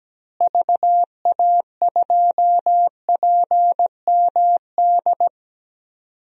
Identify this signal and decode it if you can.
Morse code, heard as a keyed tone, about 17 wpm, 700 Hz. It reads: VA2PMD
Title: DXCC-100 Call Signs - 6 Characters - Repeated 3x - 17wpm